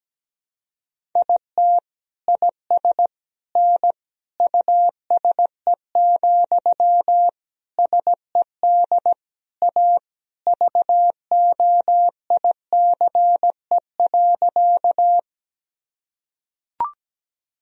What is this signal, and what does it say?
Morse code, heard as a keyed tone, about 17 wpm, 700 Hz. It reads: IT IS N USE, SED A VOICE.
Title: Sentences from Top 500 Words and Abbreviations 17wpm